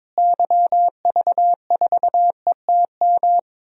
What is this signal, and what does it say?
Morse code, heard as a keyed tone, about 22 wpm, 700 Hz. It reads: YV4ETM
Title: DXCC-100 Call Signs - 6 Characters - Repeated 3x - 22wpm